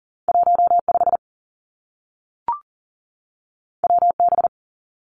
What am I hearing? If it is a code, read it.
Morse code, heard as a keyed tone, about 42 wpm, 700 Hz. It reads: 1 5 W B
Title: Two Challenging Letter-Number Combinations 40wpm